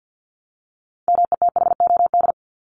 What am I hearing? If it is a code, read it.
Morse code, heard as a keyed tone, about 50 wpm, 700 Hz. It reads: METHOD
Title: Top 501-750 Words 50wpm